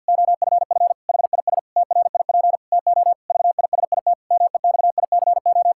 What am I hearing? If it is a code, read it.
Morse code, heard as a keyed tone, about 50 wpm, 700 Hz. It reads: OFF HIS TRIP TO VISIT MEXICO
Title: Pangram Challenge v2 - 50wpm